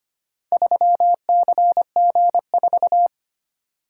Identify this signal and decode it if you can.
Morse code, heard as a keyed tone, about 25 wpm, 700 Hz. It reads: 3CG4
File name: practice / 25wpm / Four Letter-Number Combinations - 25wpm